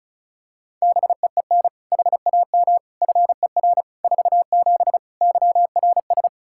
Morse code, heard as a keyed tone, about 34 wpm, 700 Hz. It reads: BEEN HAM FER 47 YRS
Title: Practice QSO - Exchange 3 - More Info - 35wpm